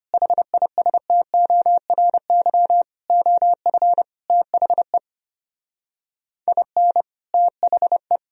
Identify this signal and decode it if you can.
Morse code, heard as a keyed tone, about 31 wpm, 700 Hz. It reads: HISTORY OF THE IN THE
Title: Speed-Racing - Sets of 5 Words 25wpm